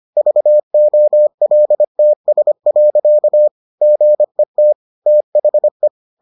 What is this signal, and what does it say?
Morse code, heard as a keyed tone, about 25 wpm, 600 Hz. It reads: VOLTS. GET THE